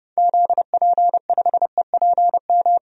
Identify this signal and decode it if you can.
Morse code, heard as a keyed tone, about 30 wpm, 700 Hz. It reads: ZP5EPM
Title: DXCC-100 Call Signs - 6 Characters - 30wpm